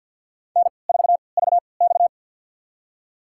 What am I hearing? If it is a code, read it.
Morse code, heard as a keyed tone, about 47 wpm, spaced as for 28 wpm, 700 Hz. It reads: N4VX
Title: USA Call Signs - 4 Characters - 3x Character Spacing - 50wpm